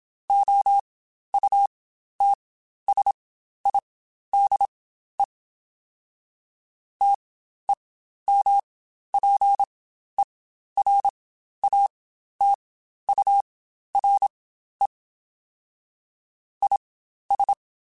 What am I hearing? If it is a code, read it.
Morse code, heard as a keyed tone, about 27 wpm, spaced as for 12 wpm, 800 Hz. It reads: OUTSIDE TEMPERATURE IS
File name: Spaced Characters / Spaced Characters-mp3s / Weather 3 17 sp @25